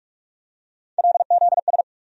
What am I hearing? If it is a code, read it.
Morse code, heard as a keyed tone, about 42 wpm, 700 Hz. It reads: LZS
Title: USA Call Sign Postfixes - 3 Letters 45wpm